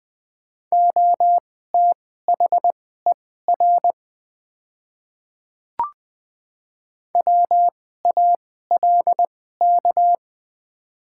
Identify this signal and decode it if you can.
Morse code, heard as a keyed tone, about 20 wpm, spaced as for 15 wpm, 700 Hz. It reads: OTHER WALK
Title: 2x Character Spacing - Top 500 Words - 20wpm